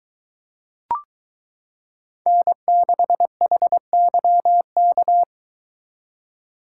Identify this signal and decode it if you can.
Morse code, heard as a keyed tone, about 22 wpm, 700 Hz. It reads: N6HYK